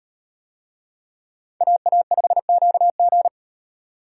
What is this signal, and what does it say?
Morse code, heard as a keyed tone, about 38 wpm, 700 Hz. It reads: AA5QG